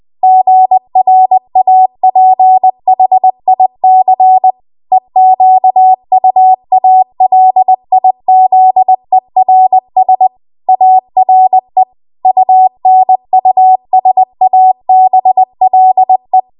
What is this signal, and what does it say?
Morse code, heard as a keyed tone, about 20 wpm, 750 Hz. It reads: GRAPHIC EQUALIZERS ARE UNUSABLE